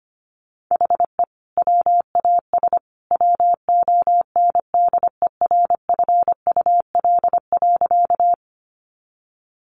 Morse code, heard as a keyed tone, about 25 wpm, 700 Hz. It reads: HE WAS WONDERFUL.
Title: Sentences from Top 1100 Words 25wpm